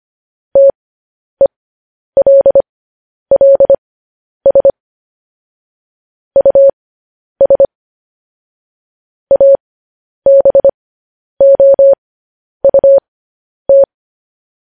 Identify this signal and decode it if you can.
Morse code, heard as a keyed tone, about 25 wpm, spaced as for 10 wpm, 550 Hz. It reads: TELLS US ABOUT